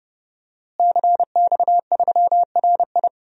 Morse code, heard as a keyed tone, about 30 wpm, 700 Hz. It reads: CX3RI
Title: DXCC-100 Call Signs - 5 Characters - Repeated 3x - 30wpm